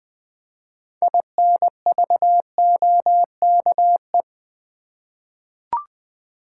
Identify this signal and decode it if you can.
Morse code, heard as a keyed tone, about 20 wpm, 700 Hz. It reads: INVOKE
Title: 6 Letter Words - First 3 Letters Given - 20wpm